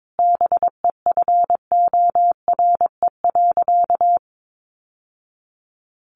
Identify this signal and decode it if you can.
Morse code, heard as a keyed tone, about 22 wpm, 700 Hz. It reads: BEFORE.
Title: Sentences from Top 1100 Words 22wpm